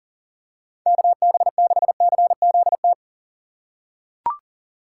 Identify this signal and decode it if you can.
Morse code, heard as a keyed tone, about 40 wpm, 700 Hz. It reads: KB6CZT
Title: USA Call Signs - 6 Characters - 6x Character Spacing - 40wpm